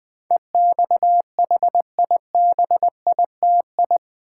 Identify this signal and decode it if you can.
Morse code, heard as a keyed tone, about 20 wpm, 700 Hz. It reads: EXHIBITI
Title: Build A Word - 4 Letters At A Time - 10-Letter Words - 20wpm